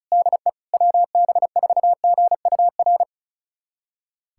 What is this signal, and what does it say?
Morse code, heard as a keyed tone, about 36 wpm, 700 Hz. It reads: DE WB4GUR
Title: USA Callsign DE Callsign 35wpm